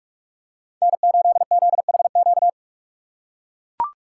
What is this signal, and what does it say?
Morse code, heard as a keyed tone, about 44 wpm, 700 Hz. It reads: N8ZHQ
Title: USA Call Signs - 5 Characters - 6x Character Spacing - 45wpm